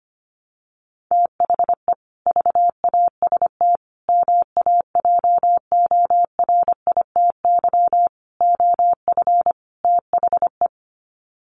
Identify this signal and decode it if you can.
Morse code, heard as a keyed tone, about 25 wpm, 700 Hz. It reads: THE VAST MAJORITY OF THE
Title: Sets of 5 Words Encore 25wpm